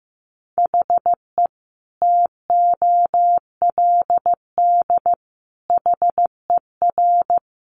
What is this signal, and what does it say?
Morse code, heard as a keyed tone, about 14 wpm, 700 Hz. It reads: HE TOLD HER
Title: Sentences from Top 700 Words 15wpm